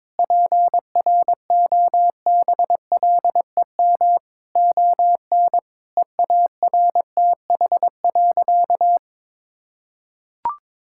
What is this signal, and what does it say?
Morse code, heard as a keyed tone, about 21 wpm, 700 Hz. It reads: PROBLEM ON EARTH.
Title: Sentences from Top 1000 Words - Review Part 6 - 22wpm